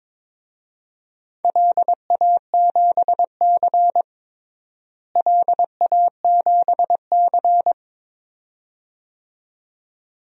Morse code, heard as a keyed tone, about 23 wpm, 700 Hz. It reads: LA7C LA7C
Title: DXCC-100 Call Signs - 4 Characters - Repeated 2x - 22wpm